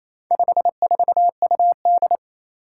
Morse code, heard as a keyed tone, about 29 wpm, 700 Hz. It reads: 54UD
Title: Four Challenging Letter-Number Combinations 28wpm